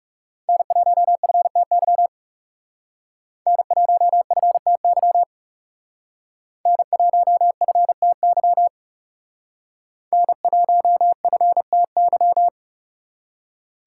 Morse code, heard as a keyed tone, about 38 wpm, 700 Hz. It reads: N1FTY N1FTY N1FTY N1FTY